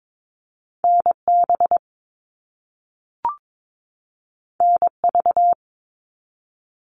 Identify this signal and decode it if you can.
Morse code, heard as a keyed tone, about 22 wpm, 700 Hz. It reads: NB NV
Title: All Cut Numbers - 2-Digit 22wpm